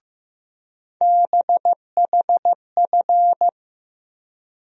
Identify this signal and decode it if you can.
Morse code, heard as a keyed tone, about 15 wpm, 700 Hz. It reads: BHF
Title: USA Call Sign Postfixes - 3 Letters 15wpm